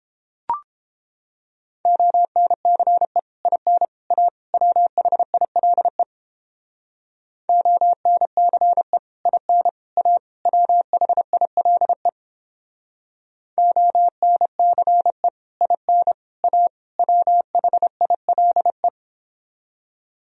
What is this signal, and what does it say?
Morse code, heard as a keyed tone, about 29 wpm, 700 Hz. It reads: ONCE IN A WHILE ONCE IN A WHILE ONCE IN A WHILE